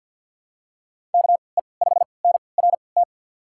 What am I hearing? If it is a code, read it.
Morse code, heard as a keyed tone, about 53 wpm, spaced as for 29 wpm, 700 Hz. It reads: KE5NRT
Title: USA Call Signs - 6 Characters - 3x Character Spacing - 50wpm